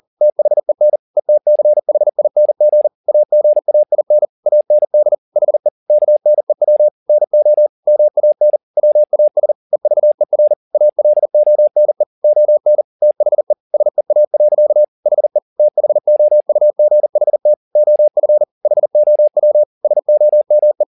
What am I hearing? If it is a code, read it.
Morse code, heard as a keyed tone, about 40 wpm, 600 Hz. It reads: THEN ETCHING AGAIN AND HE KNEW NO MAN WAS EVER ALONE ON THE SEA. HE THOUGHT OF HOW SOME